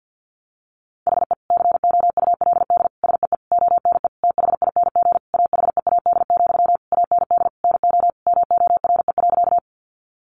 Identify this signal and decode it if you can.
Morse code, heard as a keyed tone, about 50 wpm, 700 Hz. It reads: HE COULD SEE ONE THING AHEAD, AND NO MORE.
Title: Sentences from Top 1100 Words 50wpm